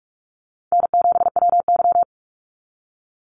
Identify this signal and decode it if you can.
Morse code, heard as a keyed tone, about 44 wpm, 700 Hz. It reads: N7WY
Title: Easy Call Signs Repeated 2x - 45wpm